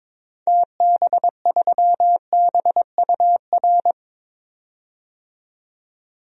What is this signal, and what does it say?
Morse code, heard as a keyed tone, about 22 wpm, 700 Hz. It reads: TB3BUR
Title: DXCC-100 Call Signs - 6 Characters - 22wpm